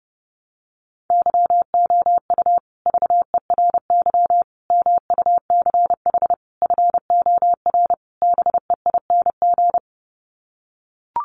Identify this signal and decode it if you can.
Morse code, heard as a keyed tone, about 30 wpm, 700 Hz. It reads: YOU VERY MUCH FOR BEING
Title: Sets of 5 Words Encore 30wpm